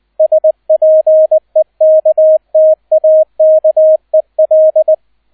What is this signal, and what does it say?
Morse code, heard as a keyed tone, about 20 wpm, 630 Hz, with radio noise. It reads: SPEKTAKEL
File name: 002 Zweiter Streich 20wpm